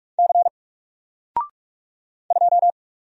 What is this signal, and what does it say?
Morse code, heard as a keyed tone, about 46 wpm, 700 Hz. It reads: C 2
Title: Single Letter-Number Alt 45wpm